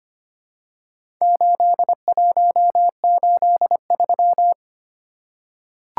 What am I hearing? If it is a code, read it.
Morse code, heard as a keyed tone, about 25 wpm, 690 Hz. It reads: 8183
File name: practice / 25wpm / Four Letter-Number Combinations - 25wpm